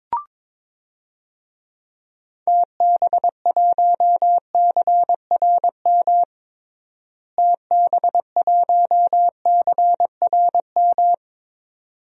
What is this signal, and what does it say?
Morse code, heard as a keyed tone, about 22 wpm, 700 Hz. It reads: TB1CRM TB1CRM
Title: DXCC-100 Call Signs - 6 Characters - Repeated 3x - 22wpm